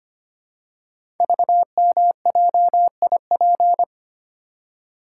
Morse code, heard as a keyed tone, about 25 wpm, 700 Hz. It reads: VMJIP
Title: Code Groups - 5 Letter:Numbers - 25wpm